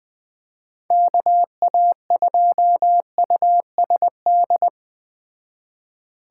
Morse code, heard as a keyed tone, about 20 wpm, 700 Hz. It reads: KA2USD
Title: Speed Racing - USA 2x3 Call Signs 17wpm